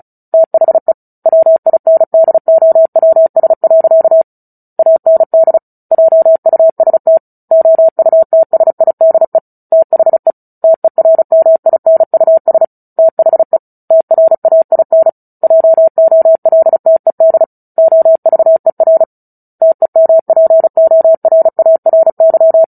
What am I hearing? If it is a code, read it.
Morse code, heard as a keyed tone, about 34 wpm, 650 Hz. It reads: THE WINDOWS. AND JUST OUTSIDE THE TERMINUS THE TRAIN JOLTED OVER TEMPORARY
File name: Book Two / 09-Chapter 9 - 35 WPM